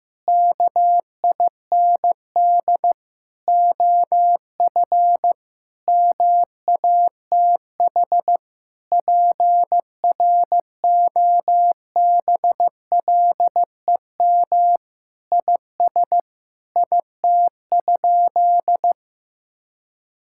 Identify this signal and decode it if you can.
Morse code, heard as a keyed tone, about 15 wpm, 700 Hz. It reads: KIND OF MATH PROBLEM IS IT?
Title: Sentences from Top 1000 Words - Review Part 6 - 15wpm